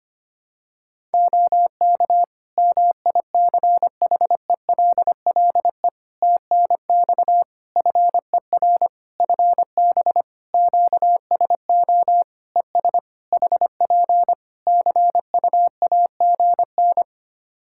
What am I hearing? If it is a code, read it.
Morse code, heard as a keyed tone, about 25 wpm, 700 Hz. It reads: OK MICHELLE TNX FER FB QSO ES HP CUAGN
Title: Practice QSO - Exchange 4 - Ending - 25wpm